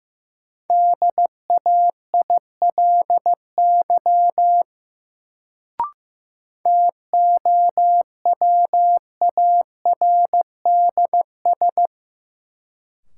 Morse code, heard as a keyed tone, about 15 wpm, 700 Hz. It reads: DAILY TOWARDS
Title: Sentences from Top 1100 Words - New Word Drill Alt - 15wpm